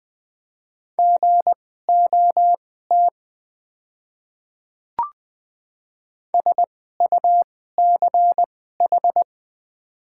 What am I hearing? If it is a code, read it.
Morse code, heard as a keyed tone, about 20 wpm, spaced as for 14 wpm, 700 Hz. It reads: GOT SUCH
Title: 2x Character Spacing - Top 500 Words - 20wpm